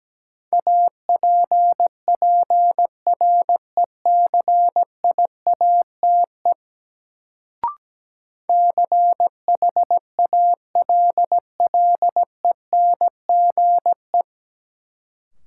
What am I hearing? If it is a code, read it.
Morse code, heard as a keyed tone, about 17 wpm, 700 Hz. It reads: APPRECIATE CHALLENGE
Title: Sentences from Top 1300 Words - New Word Drill Alt - 17wpm